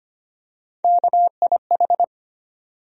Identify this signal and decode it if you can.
Morse code, heard as a keyed tone, about 25 wpm, 700 Hz. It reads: KIH
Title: USA Call Sign Postfixes - 3 Letters 25wpm